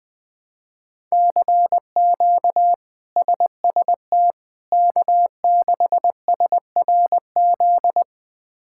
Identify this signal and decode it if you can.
Morse code, heard as a keyed tone, about 20 wpm, 700 Hz. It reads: CQ SST K6SRZ
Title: K1USN SST QSO Practice - 20wpm